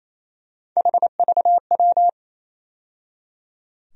Morse code, heard as a keyed tone, about 28 wpm, 700 Hz. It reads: HVW
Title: USA Call Sign Postfixes - 3 Letters 28wpm